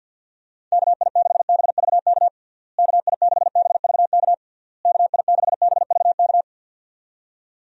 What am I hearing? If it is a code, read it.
Morse code, heard as a keyed tone, about 51 wpm, 700 Hz. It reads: KI6BVK KI6BVK KI6BVK
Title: USA Call Signs - 6 Characters - Repeated 3x - 50wpm